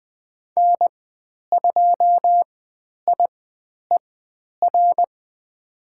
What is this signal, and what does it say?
Morse code, heard as a keyed tone, about 20 wpm, spaced as for 10 wpm, 700 Hz. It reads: N2IER
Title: 102 Lesson 17g - Letter u - Call Signs 20wpm